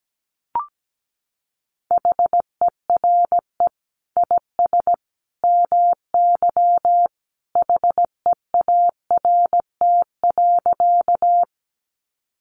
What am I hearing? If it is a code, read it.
Morse code, heard as a keyed tone, about 17 wpm, 700 Hz. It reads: HERE IS MY HEART.
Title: Sentences from Top 1200 Words 17wpm